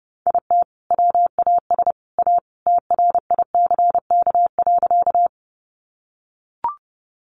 Morse code, heard as a keyed tone, about 30 wpm, 700 Hz. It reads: IT WAS A TRICK.
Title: Sentences from Top 1500 Words 30wpm